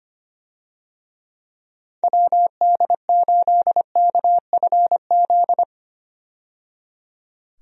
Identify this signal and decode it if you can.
Morse code, heard as a keyed tone, about 25 wpm, 700 Hz. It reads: WD8KFZ
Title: USA Call Signs - 6 Characters - 25wpm